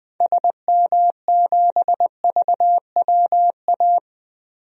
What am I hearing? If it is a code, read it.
Morse code, heard as a keyed tone, about 20 wpm, 700 Hz. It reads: SM7VWA
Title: DXCC-100 Call Signs - 6 Characters - 20wpm